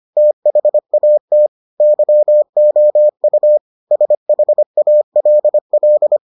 Morse code, heard as a keyed tone, about 25 wpm, 600 Hz. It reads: THAT YOU SHALL